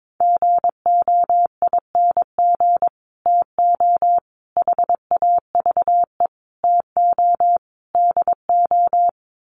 Morse code, heard as a keyed tone, about 22 wpm, 700 Hz. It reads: GOING TO HAVE TO DO
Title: Sets of 5 Words Encore 22wpm